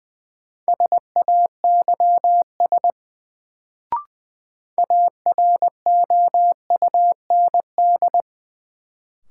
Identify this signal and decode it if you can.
Morse code, heard as a keyed tone, about 20 wpm, 700 Hz. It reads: SAYS AROUND
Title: Sets of 2 Words - New Word Drill - 20wpm